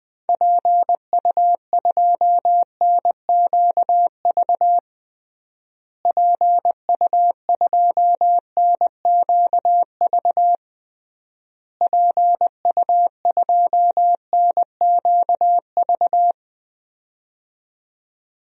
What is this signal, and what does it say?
Morse code, heard as a keyed tone, about 20 wpm, 700 Hz. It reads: PU2NQV PU2NQV PU2NQV
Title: DXCC-100 Call Signs - 6 Characters - Repeated 3x - 20wpm